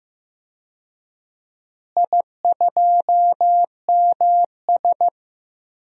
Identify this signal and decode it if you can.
Morse code, heard as a keyed tone, about 15 wpm, 700 Hz. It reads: I2MS
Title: DXCC-100 Call Signs - 4 Characters - Repeated 2x - 15wpm